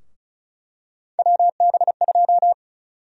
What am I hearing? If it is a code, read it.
Morse code, heard as a keyed tone, about 36 wpm, 700 Hz. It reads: WB2
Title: USA Call Sign Prefixes - Repeated 2x - 35wpm